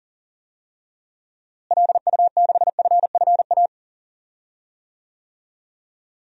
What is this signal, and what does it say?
Morse code, heard as a keyed tone, about 37 wpm, 700 Hz. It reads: LU6FFA
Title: DXCC-100 Call Signs - 6 Characters - Repeated 3x - 40wpm